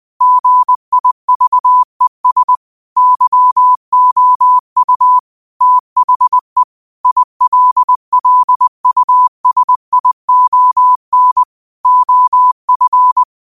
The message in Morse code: GIVES YOU THE ILLUSION OF

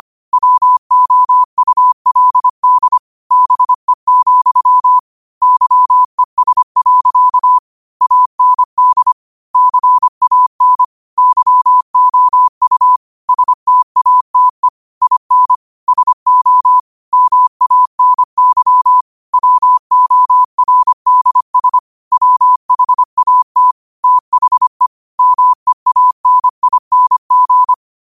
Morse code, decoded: WOULD BE, YES. AND CAN YOU STATE IN SO MANY WORDS WHAT THE MEANING

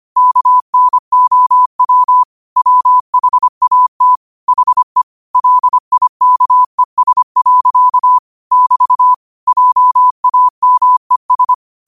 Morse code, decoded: KNOW WHAT HE LIKES. <BT> JAMES

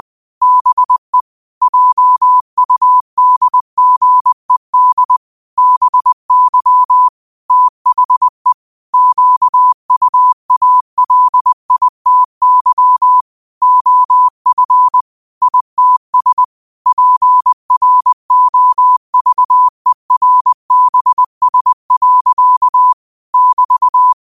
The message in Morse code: BE JUDGED BY THE QUALITY OF ITS PROVERBS. <BT>

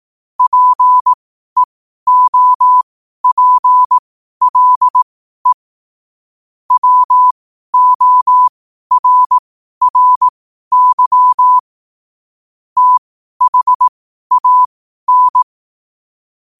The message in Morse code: PEOPLE WORRY THAN

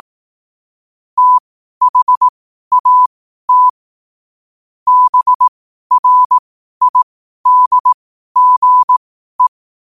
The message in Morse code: THAT BRIDGE

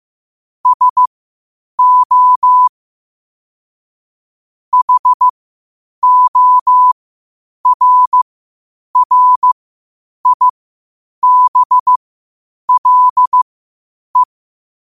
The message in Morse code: SO HORRIBLE

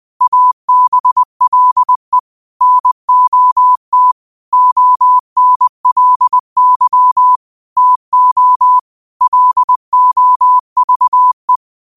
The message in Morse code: ABLE NOT ONLY TO LOVE

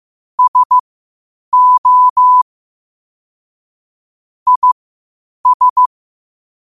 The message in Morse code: SO IS

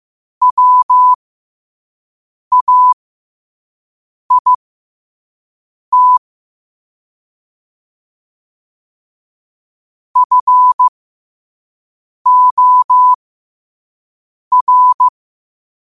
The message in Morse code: WAIT FOR

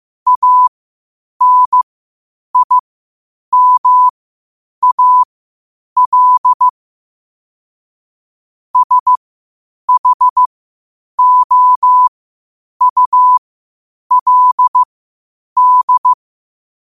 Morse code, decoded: ANIMAL SHOULD

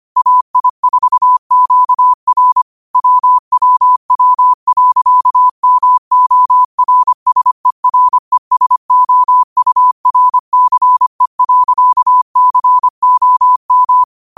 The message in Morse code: AI4QR WWW.MORSERESOURCE.COM

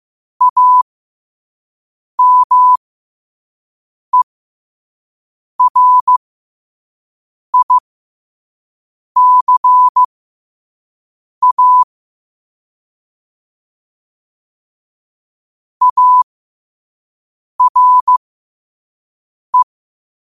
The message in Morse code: AMERICA ARE

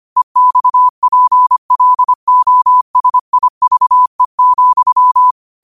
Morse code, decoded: EXPLOSIVE,

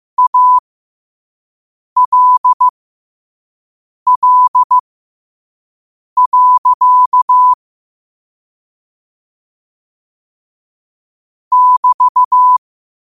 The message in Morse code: ALL. <BT>

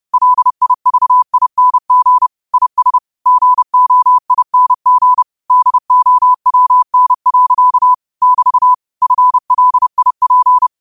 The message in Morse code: LIVING IS GOING DOWN. <BT> FLIP

